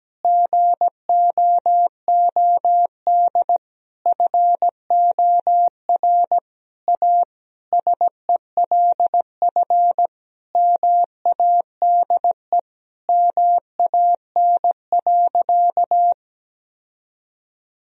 Morse code, heard as a keyed tone, about 17 wpm, 700 Hz. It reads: GOOD FOR A SELF MADE MAN.
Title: Sentences from Top 1000 Words - Review Part 1 - 17wpm